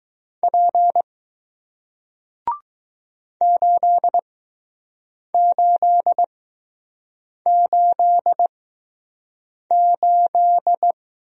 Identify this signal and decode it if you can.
Morse code, heard as a keyed tone, about 20 wpm, spaced as for 6 wpm, 700 Hz. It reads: P 8888